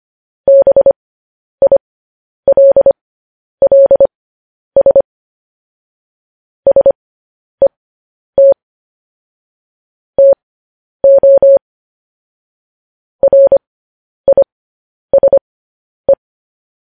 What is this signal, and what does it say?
Morse code, heard as a keyed tone, about 25 wpm, spaced as for 10 wpm, 550 Hz. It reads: BILLS SET TO RISE
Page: News Headlines in Morse Code at 10 WPM